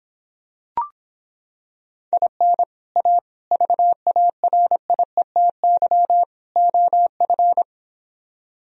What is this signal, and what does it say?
Morse code, heard as a keyed tone, about 26 wpm, 700 Hz. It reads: IN A VARIETY OF